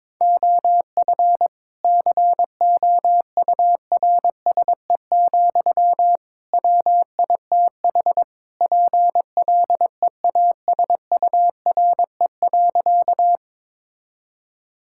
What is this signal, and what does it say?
Morse code, heard as a keyed tone, about 22 wpm, 700 Hz. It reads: OF COURSE, WITH PLEASURE.
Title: Sentences from Top 1600 Words 22wpm